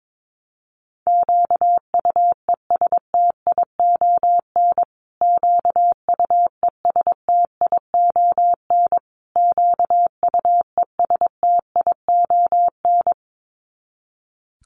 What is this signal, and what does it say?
Morse code, heard as a keyed tone, about 22 wpm, 700 Hz. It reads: QUESTION QUESTION QUESTION
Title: Top 500 Words Repeated 22wpm